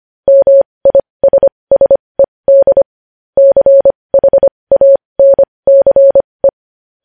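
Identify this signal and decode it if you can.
Morse code, heard as a keyed tone, about 25 wpm, spaced as for 20 wpm, 550 Hz. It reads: MISSED CHANCE
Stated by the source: Morse code news headlines practice at 20 wpm.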